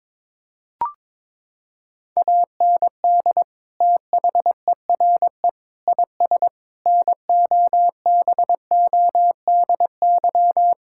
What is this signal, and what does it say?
Morse code, heard as a keyed tone, about 22 wpm, 700 Hz. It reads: AND THERE IS NOBODY
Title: Sentences from Top 1600 Words 22wpm